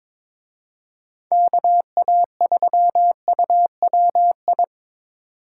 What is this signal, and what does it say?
Morse code, heard as a keyed tone, about 26 wpm, 700 Hz. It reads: KA3UWI
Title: Speed Racing - USA 2x3 Call Signs 22wpm